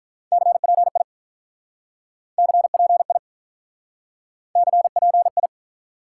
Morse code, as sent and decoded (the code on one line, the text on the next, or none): -.-. .--. .. / -.-. .--. .. / -.-. .--. ..
CPI CPI CPI